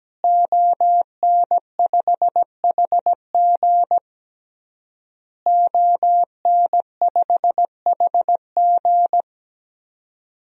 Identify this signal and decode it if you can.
Morse code, heard as a keyed tone, about 17 wpm, 700 Hz. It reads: ON5HG ON5HG
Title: DXCC-100 Call Signs - 5 Characters - Repeated 3x - 17wpm